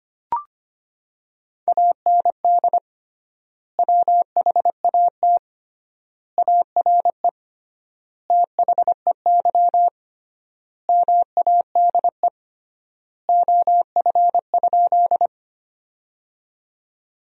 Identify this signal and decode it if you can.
Morse code, heard as a keyed tone, about 25 wpm, 700 Hz. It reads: AND WHAT ARE THEY MADE OF?
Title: 3x Word-Spacing - Sentences from Top 200 Words - 25wpm